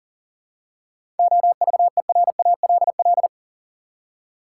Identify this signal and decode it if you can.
Morse code, heard as a keyed tone, about 40 wpm, 700 Hz. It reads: OVERALL
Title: Seven Letter Words - 40wpm